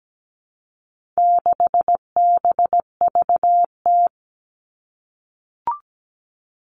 Morse code, heard as a keyed tone, about 17 wpm, 700 Hz. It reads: 6BVT
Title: All Cut Numbers - 4-Digit 17wpm